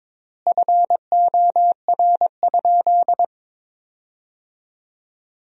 Morse code, heard as a keyed tone, about 23 wpm, 700 Hz. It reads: FOR?
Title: 3x Word-Spacing - Sentences from Top 400 Words - 22wpm